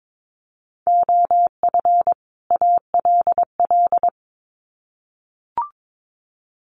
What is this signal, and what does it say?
Morse code, heard as a keyed tone, about 22 wpm, 700 Hz. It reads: OF ALL
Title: Sets of 2 Words 22wpm